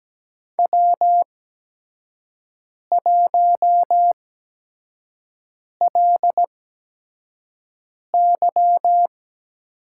Morse code, heard as a keyed tone, about 17 wpm, spaced as for 5 wpm, 700 Hz. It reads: W1LY
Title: USA Call Signs - 8x Character Spacing - 17wpm